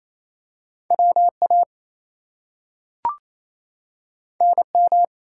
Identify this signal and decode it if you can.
Morse code, heard as a keyed tone, about 28 wpm, 700 Hz. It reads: WA NM
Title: US States and Canadian Provinces 28wpm